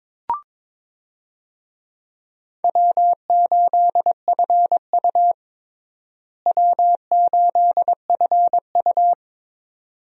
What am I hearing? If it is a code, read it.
Morse code, heard as a keyed tone, about 22 wpm, 700 Hz. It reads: W8FU W8FU
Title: USA Call Signs - 4 Characters - Repeated 2x - 22wpm